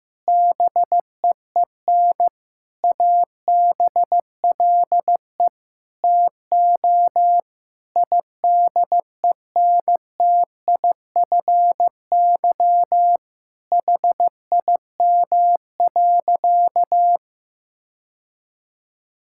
Morse code, heard as a keyed tone, about 15 wpm, 700 Hz. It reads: BEEN ABLE TO IDENTIFY HIM.
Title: Sentences from Top 1000 Words - Review Part 6 - 15wpm